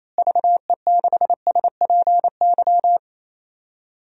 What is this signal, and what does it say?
Morse code, heard as a keyed tone, about 28 wpm, 700 Hz. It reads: VE6SPY
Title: DXCC-100 Call Signs - 6 Characters - Repeated 2x - 28wpm